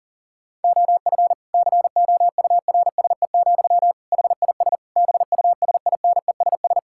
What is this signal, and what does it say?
Morse code, heard as a keyed tone, about 40 wpm, 700 Hz. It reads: OF COURSE, HIS BUSINESS
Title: Sentences from Top 1000 Words - Review Part 5 - 40wpm